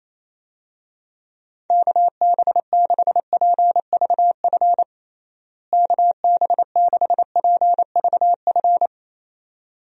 Morse code, heard as a keyed tone, about 28 wpm, 700 Hz. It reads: KB6PVF KB6PVF
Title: USA Call Signs - 6 Characters - Repeated 2x - 28wpm